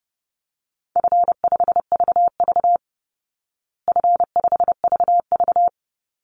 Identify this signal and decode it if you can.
Morse code, heard as a keyed tone, about 30 wpm, 700 Hz. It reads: F5VV F5VV
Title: Easy Call Signs Repeated 3x - 30wpm